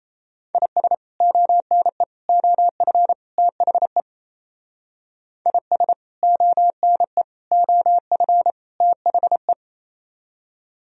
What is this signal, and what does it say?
Morse code, heard as a keyed tone, about 35 wpm, 700 Hz. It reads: IS ONE OF THE IS ONE OF THE